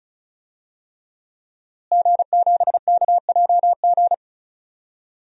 Morse code, heard as a keyed tone, about 36 wpm, 700 Hz. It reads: G7KJG
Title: DXCC-100 Call Signs - 5 Characters - Repeated 3x - 35wpm